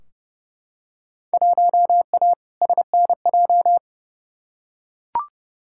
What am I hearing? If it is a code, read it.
Morse code, heard as a keyed tone, about 30 wpm, 700 Hz. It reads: 1A SNJ
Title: Field Day Exchanges - 30wpm